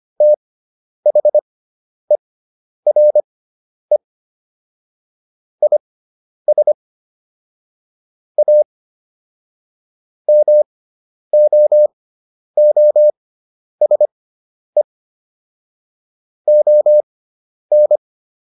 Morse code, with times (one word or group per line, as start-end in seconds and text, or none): 0.2-4.0 THERE
5.6-6.7 IS
8.4-8.6 A
10.3-14.8 MOOSE
16.5-18.0 ON